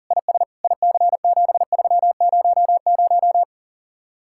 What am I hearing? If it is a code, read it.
Morse code, heard as a keyed tone, about 50 wpm, 700 Hz. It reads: IS IC7300